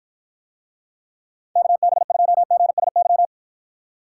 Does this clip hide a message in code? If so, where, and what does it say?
Yes: Morse code from 1.6 to 3.3 s, heard as a keyed tone, about 52 wpm, 700 Hz. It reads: KB2GSY